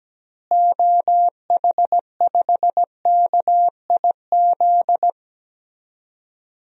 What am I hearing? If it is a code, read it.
Morse code, heard as a keyed tone, about 17 wpm, 700 Hz. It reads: OH5KIZ
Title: DXCC-100 Call Signs - 6 Characters - Repeated 3x - 17wpm